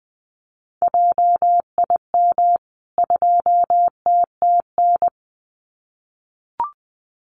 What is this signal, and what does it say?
Morse code, heard as a keyed tone, about 20 wpm, 700 Hz. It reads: JIM 2TTN
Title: CWT Exchange 20wpm